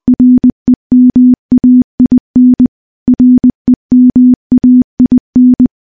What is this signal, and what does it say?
Morse code, heard as a keyed tone, about 21 wpm, 250 Hz. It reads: REMAIN REMAIN
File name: morse-training-20WPM